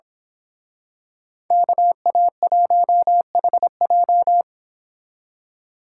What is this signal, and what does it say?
Morse code, heard as a keyed tone, about 26 wpm, 700 Hz. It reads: KA1HJ
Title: Speed Racing - USA 2x2 Call Signs 22wpm